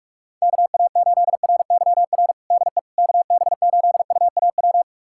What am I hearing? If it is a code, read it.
Morse code, heard as a keyed tone, about 46 wpm, 700 Hz. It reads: KA8RYR DE KB8UAW
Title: USA Callsign DE Callsign 45wpm